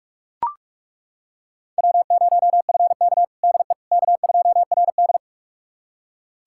Morse code, heard as a keyed tone, about 41 wpm, 700 Hz. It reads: W0FK DE K2RD
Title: USA Callsign DE Callsign - 4 Characters - 45wpm